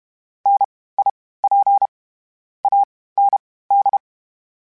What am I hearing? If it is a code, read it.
Morse code, heard as a keyed tone, about 32 wpm, spaced as for 18 wpm, 800 Hz. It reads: NIP AND
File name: Misc / Misc-mp3s / WordPairs sp @30